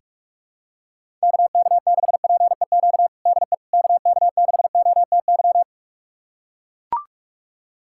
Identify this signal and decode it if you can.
Morse code, heard as a keyed tone, about 43 wpm, 700 Hz. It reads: KK6PEQ DE KK6OTY
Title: USA Callsign DE Callsign - 6 Characters - 45wpm